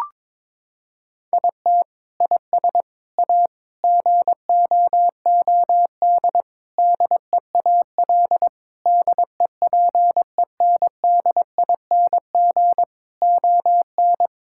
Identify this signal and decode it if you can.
Morse code, heard as a keyed tone, about 22 wpm, 700 Hz. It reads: IT IS A GOOD DEAL DEPENDING ON
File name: Sentences with 7-Word Phrases - 22wpm